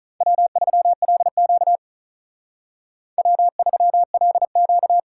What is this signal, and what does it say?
Morse code, heard as a keyed tone, about 39 wpm, 700 Hz. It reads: W3LQ W3LQ
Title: Speed Racing - USA 1x2 Call Signs 35wpm